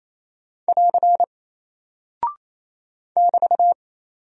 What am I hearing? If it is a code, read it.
Morse code, heard as a keyed tone, about 28 wpm, 700 Hz. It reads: <AR> <BT>
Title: Single Character (Letter:Number:Punctuation:Prosign) Alt 28wpm